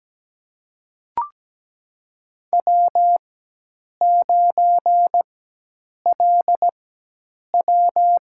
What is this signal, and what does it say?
Morse code, heard as a keyed tone, about 17 wpm, spaced as for 8 wpm, 700 Hz. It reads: W9LW
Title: USA Call Signs - 4x Character Spacing - 17wpm